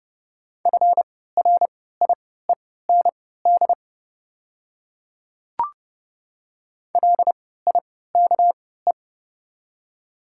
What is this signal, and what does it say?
Morse code, heard as a keyed tone, about 30 wpm, spaced as for 17 wpm, 700 Hz. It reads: FRIEND LIKE
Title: 3x Character Spacing - Top 300 Words - 30wpm